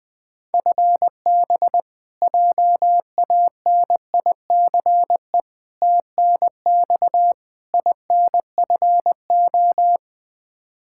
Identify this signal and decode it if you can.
Morse code, heard as a keyed tone, about 20 wpm, 700 Hz. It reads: FB JANICE TNX INFO
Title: K1USN SST QSO Practice 20wpm